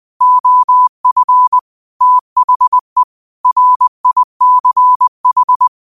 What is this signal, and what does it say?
Morse code, heard as a keyed tone, about 20 wpm, 1000 Hz. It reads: OF THE RICH
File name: qotdfiles / 20wpm